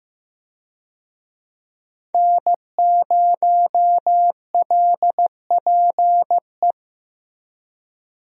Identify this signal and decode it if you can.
Morse code, heard as a keyed tone, about 16 wpm, 700 Hz. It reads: N0LPE
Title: USA Call Signs - 5 Characters - Repeated 2x - 15wpm